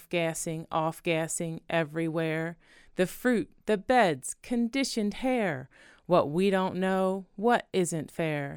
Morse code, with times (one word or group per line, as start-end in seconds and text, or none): none